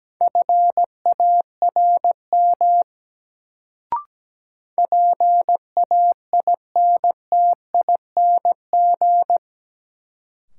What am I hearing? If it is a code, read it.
Morse code, heard as a keyed tone, about 17 wpm, 700 Hz. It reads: FARM PAINTING
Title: Sentences from Top 1200 Words - New Word Drill Alt - 17wpm